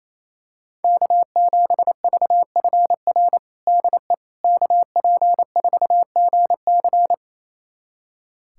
Morse code, heard as a keyed tone, about 28 wpm, 700 Hz. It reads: K7VFR DE KP4GC
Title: USA Callsign DE Callsign - 5 Characters - 28wpm